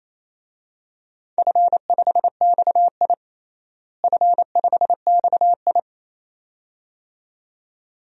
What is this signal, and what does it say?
Morse code, heard as a keyed tone, about 28 wpm, 700 Hz. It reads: F5XI F5XI
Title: DXCC-100 Call Signs - 4 Characters - Repeated 2x - 28wpm